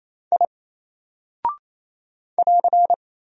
Morse code, heard as a keyed tone, about 28 wpm, 700 Hz. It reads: I <AR>
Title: Single Character (Letter:Number:Punctuation:Prosign) 28wpm